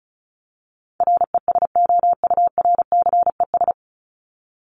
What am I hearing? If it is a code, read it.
Morse code, heard as a keyed tone, about 34 wpm, 700 Hz. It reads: RESOURCES
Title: ES:S Suffix - Top 100 Words with Base Words 35wpm